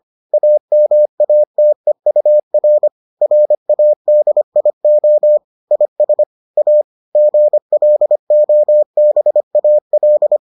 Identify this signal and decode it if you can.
Morse code, heard as a keyed tone, about 25 wpm, 600 Hz. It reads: AMATEUR RADIO IS A GLOBAL